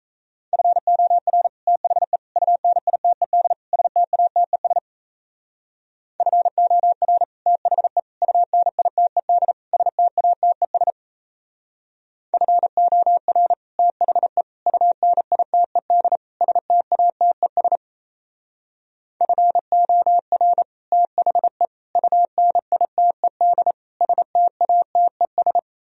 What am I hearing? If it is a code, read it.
Morse code, heard as a keyed tone, about 35 wpm, 700 Hz. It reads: FOR THE UNITED STATES FOR THE UNITED STATES FOR THE UNITED STATES FOR THE UNITED STATES